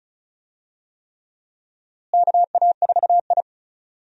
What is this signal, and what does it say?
Morse code, heard as a keyed tone, about 36 wpm, 700 Hz. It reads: KA4I
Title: USA Call Signs - 4 Characters - Repeated 3x - 35wpm